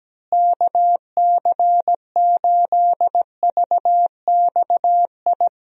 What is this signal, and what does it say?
Morse code, heard as a keyed tone, about 17 wpm, 700 Hz. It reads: KC8VXI
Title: USA Call Signs - 6 Characters - Repeated 3x - 17wpm